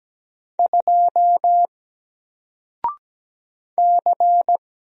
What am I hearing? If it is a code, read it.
Morse code, heard as a keyed tone, about 17 wpm, 700 Hz. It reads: 2 C
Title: Single Character (Letter:Number:Punctuation:Prosign) 17wpm